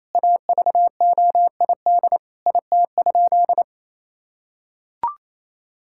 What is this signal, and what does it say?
Morse code, heard as a keyed tone, about 28 wpm, 700 Hz. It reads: AVOID IT?
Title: Sentences from Top 1000 Words - Review Part 6 - 28wpm